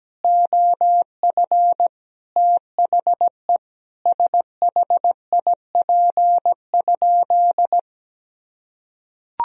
Morse code, heard as a keyed tone, about 17 wpm, 700 Hz. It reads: OF THE SHIP?
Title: Sentences from Top 1000 Words - Review Part 2 - 17wpm